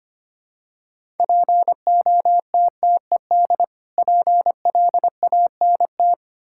Morse code, heard as a keyed tone, about 25 wpm, 700 Hz. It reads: POTTED PLANT
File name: practice / 25wpm / Difficult Words with Hints - 25wpm